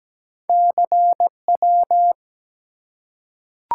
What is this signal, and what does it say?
Morse code, heard as a keyed tone, about 17 wpm, 700 Hz. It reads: CW
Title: Top 180 QSO Elements 17wpm